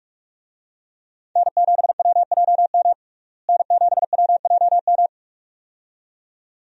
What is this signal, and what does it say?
Morse code, heard as a keyed tone, about 46 wpm, 700 Hz. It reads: N7WJM N7WJM
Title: USA Call Signs - 5 Characters - Repeated 2x - 45wpm